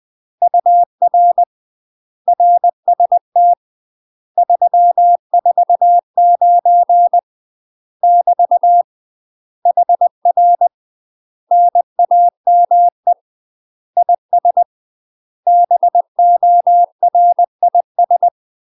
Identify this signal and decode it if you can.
Morse code, heard as a keyed tone, about 20 wpm, 700 Hz. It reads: UR RST 349 <BT> HR NAME IS BORIS
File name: UK MASTER QSOs TRAINING / 20WPM / UK MASTER QSOs 0003